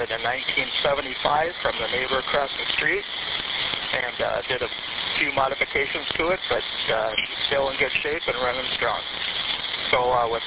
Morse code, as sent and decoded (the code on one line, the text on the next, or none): none